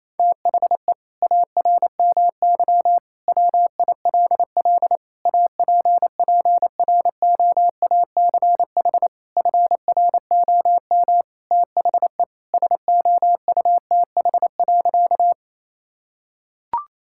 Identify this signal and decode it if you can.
Morse code, heard as a keyed tone, about 28 wpm, 700 Hz. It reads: THE ARMY WILL APPROACH FROM THE SOUTH.
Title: Sentences from Top 1000 Words - Review Part 6 - 28wpm